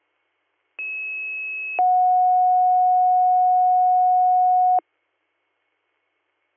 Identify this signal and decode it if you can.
Two-tone page, heard as a one-second tone, then a three-second tone, about 2600 then 730 Hz. It is heard from 0.8 to 4.8 s.